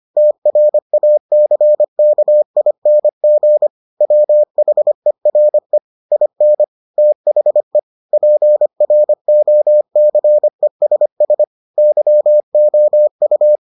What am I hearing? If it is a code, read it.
Morse code, heard as a keyed tone, about 25 wpm, 600 Hz. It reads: TRACKING WHERE IN THE PROCESS YOU